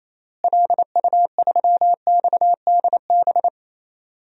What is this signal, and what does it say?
Morse code, heard as a keyed tone, about 28 wpm, 700 Hz. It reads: LU3XDB
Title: DXCC-100 Call Signs - 6 Characters - Repeated 3x - 28wpm